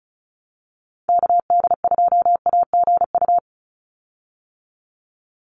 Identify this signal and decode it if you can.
Morse code, heard as a keyed tone, about 34 wpm, 700 Hz. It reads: KD2AGU
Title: USA Call Signs 35wpm